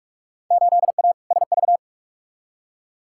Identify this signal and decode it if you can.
Morse code, heard as a keyed tone, about 44 wpm, 700 Hz. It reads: 8A SV
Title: Field Day Exchanges Encore - 45wpm